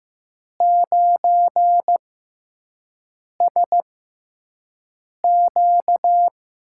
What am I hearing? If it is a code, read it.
Morse code, heard as a keyed tone, about 15 wpm, spaced as for 5 wpm, 700 Hz. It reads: 9SQ
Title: USA Call Signs - 4 Characters - 6x Character Spacing - 15wpm